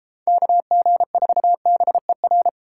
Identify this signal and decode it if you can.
Morse code, heard as a keyed tone, about 33 wpm, 700 Hz. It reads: KG4BER